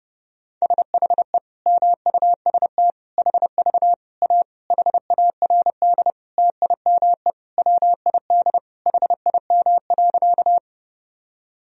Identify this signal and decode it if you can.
Morse code, heard as a keyed tone, about 30 wpm, 700 Hz. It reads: SHE MUST HV A HARD TIME WID HIM.
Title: Sentences from Top 300 Words and Abbreviations 30wpm